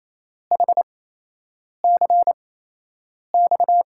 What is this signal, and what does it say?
Morse code, heard as a keyed tone, about 28 wpm, spaced as for 8 wpm, 700 Hz. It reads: HCX
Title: USA Call Signs - 8x Character Spacing - 28wpm